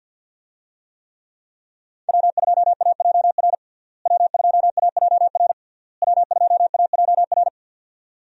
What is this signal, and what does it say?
Morse code, heard as a keyed tone, about 46 wpm, 700 Hz. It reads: W2AJR W2AJR W2AJR
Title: USA Call Signs - 5 Characters - Repeated 3x - 50wpm